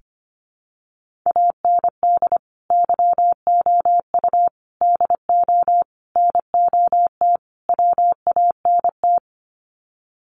Morse code, heard as a keyed tone, about 25 wpm, 700 Hz. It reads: AND YOU DO NOT WANT
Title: Sets of 5 Words Encore2 25wpm